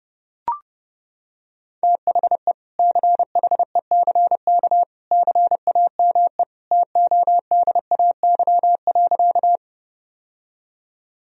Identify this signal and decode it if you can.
Morse code, heard as a keyed tone, about 30 wpm, 700 Hz. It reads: THE CHECK CAME TODAY.
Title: Sentences from Top 1000 Words - Review Part 3 - 30wpm